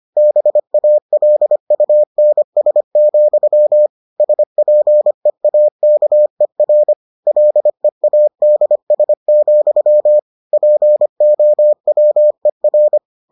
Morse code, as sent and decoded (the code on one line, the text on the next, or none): -... .- .-.. ..- -. ... --..-- / ... .--. .- -.- . .-. / .-.. .- -.. ... --..-- / .--. --- .-- . .-.
BALUNS, SPEAKER LEADS, POWER